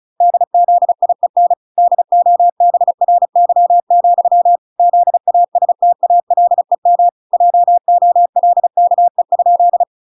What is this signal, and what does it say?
Morse code, heard as a keyed tone, about 35 wpm, 700 Hz. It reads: DZIEN DOBRY, ZASTALEM JOLKE?